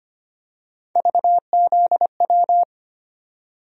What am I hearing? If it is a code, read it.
Morse code, heard as a keyed tone, about 25 wpm, 700 Hz. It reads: VZW
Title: USA Call Sign Postfixes - 3 Letters 25wpm